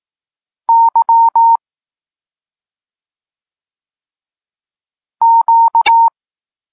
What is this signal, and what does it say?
Morse code, heard as a keyed tone, about 18 wpm, 920 Hz. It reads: Y Q
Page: Tekst 32 – Doskonalenie odbioru poznanych liter